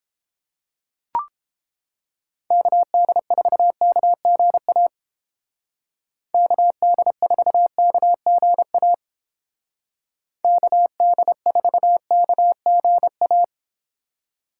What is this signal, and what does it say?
Morse code, heard as a keyed tone, about 29 wpm, 700 Hz. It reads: KD4KGA KD4KGA KD4KGA